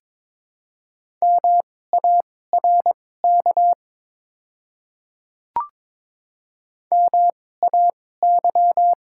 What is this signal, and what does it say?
Morse code, heard as a keyed tone, about 22 wpm, spaced as for 16 wpm, 700 Hz. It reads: MARK MAY
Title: 2x Character Spacing - Top 300 Words - 22wpm